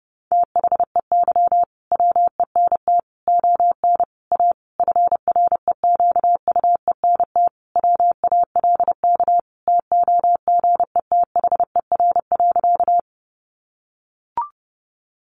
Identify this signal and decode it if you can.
Morse code, heard as a keyed tone, about 30 wpm, 700 Hz. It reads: THEY WENT ON A FREQUENT WALK TOGETHER.
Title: Sentences from Top 1400 Words 30wpm